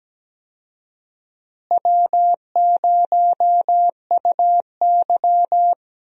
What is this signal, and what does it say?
Morse code, heard as a keyed tone, about 17 wpm, 700 Hz. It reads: W0UY
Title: USA Call Signs - 4 Characters - Repeated 3x - 17wpm